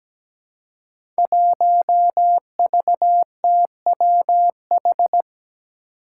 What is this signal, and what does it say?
Morse code, heard as a keyed tone, about 17 wpm, 700 Hz. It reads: 1VTWH
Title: Five Challenging Letter-Number Combinations 17wpm